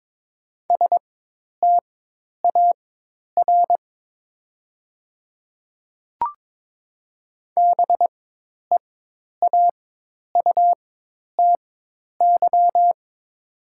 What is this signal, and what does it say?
Morse code, heard as a keyed tone, about 22 wpm, spaced as for 10 wpm, 700 Hz. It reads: STAR BEAUTY
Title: 4x Character Spacing - Top 500 Words - 22wpm